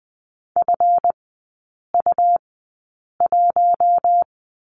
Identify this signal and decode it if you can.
Morse code, heard as a keyed tone, about 20 wpm, spaced as for 8 wpm, 700 Hz. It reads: FU1
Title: 138 Lesson 23e - Letter f - Three Characters 20wpm